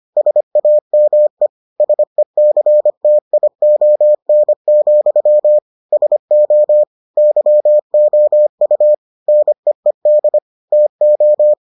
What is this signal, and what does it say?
Morse code, heard as a keyed tone, about 25 wpm, 600 Hz. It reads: SAME SECTION, SO YOU NEED TO